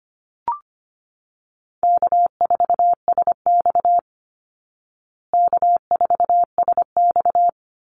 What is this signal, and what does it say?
Morse code, heard as a keyed tone, about 25 wpm, 700 Hz. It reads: K4SX K4SX
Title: Easy Call Signs Repeated 2x - 25wpm